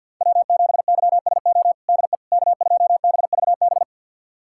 Morse code, heard as a keyed tone, about 52 wpm, 700 Hz. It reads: W7YSO DE K2BVB
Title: USA Callsign DE Callsign - 5 Characters - 50wpm